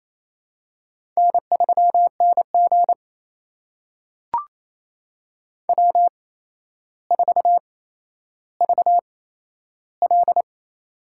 Morse code, heard as a keyed tone, about 28 wpm, spaced as for 12 wpm, 700 Hz. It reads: N3NG W4VL
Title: USA Call Signs - 4 Characters - 8x Character Spacing - 28wpm